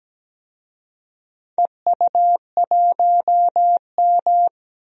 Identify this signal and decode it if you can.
Morse code, heard as a keyed tone, about 17 wpm, 700 Hz. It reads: EU1M
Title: DXCC-100 Call Signs - 4 Characters - Repeated 2x - 17wpm